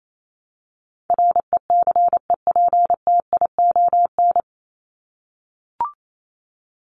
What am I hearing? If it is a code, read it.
Morse code, heard as a keyed tone, about 28 wpm, 700 Hz. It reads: RECEPTION
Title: Top 1751-2000 Words 28wpm